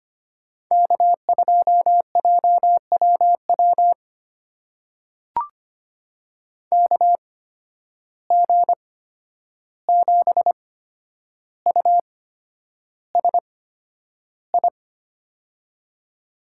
Morse code, heard as a keyed tone, about 25 wpm, spaced as for 10 wpm, 700 Hz. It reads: K2JWW KG7USI
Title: USA Call Signs - 8x Character Spacing - 25wpm